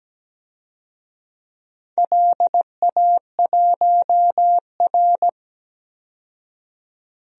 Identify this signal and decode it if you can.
Morse code, heard as a keyed tone, about 17 wpm, 700 Hz. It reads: LA1R
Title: DXCC-100 Call Signs - 4 Characters - 17wpm